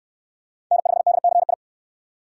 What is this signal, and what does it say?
Morse code, heard as a keyed tone, about 61 wpm, 700 Hz. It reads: N5DCI